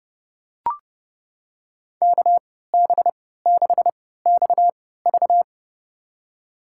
Morse code, heard as a keyed tone, about 30 wpm, spaced as for 17 wpm, 700 Hz. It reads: KB6XV
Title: USA Call Signs - 5 Characters - 3x Character Spacing - 30wpm